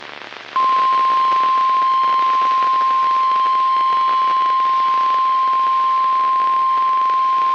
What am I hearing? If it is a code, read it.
Emergency Alert System (SAME) tones (the 1050 Hz attention tone). It is heard from 0.7 to 7.3 s.
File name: eas messages / EAS WXL38 Binghamton